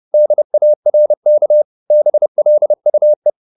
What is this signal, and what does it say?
Morse code, heard as a keyed tone, about 30 wpm, 600 Hz. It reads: DARK BLUE